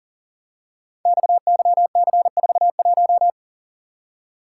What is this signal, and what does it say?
Morse code, heard as a keyed tone, about 38 wpm, 690 Hz. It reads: XYC41
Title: Code Groups - 5 Letter:Numbers - 40wpm